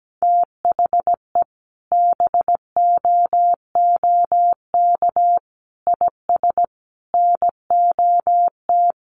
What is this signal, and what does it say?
Morse code, heard as a keyed tone, about 17 wpm, 700 Hz. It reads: THE BOOK IS NOT
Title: Sentences from Top 1400 Words 17wpm